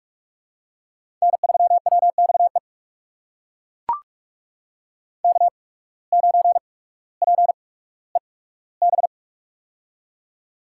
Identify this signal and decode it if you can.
Morse code, heard as a keyed tone, about 42 wpm, spaced as for 19 wpm, 700 Hz. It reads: N3WXE K9PEB
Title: USA Call Signs - 5 Characters - 8x Character Spacing - 45wpm